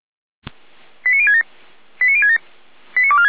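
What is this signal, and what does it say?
Five-tone selective call (ZVEI), heard as quick runs of five tones, tones of about 70 ms. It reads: 89067 89067 89246
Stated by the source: ZVEI five-tone selective call received on an SDR.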